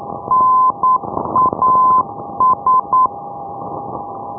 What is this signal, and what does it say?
Morse code, heard as a keyed tone, about 9 wpm, 1020 Hz, with noise. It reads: NAS
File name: ndb punta arenas